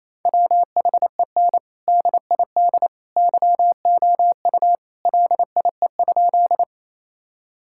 Morse code, heard as a keyed tone, about 28 wpm, 700 Hz. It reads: WHEN DID YOU LIE?
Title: Sentences from Top 1600 Words 28wpm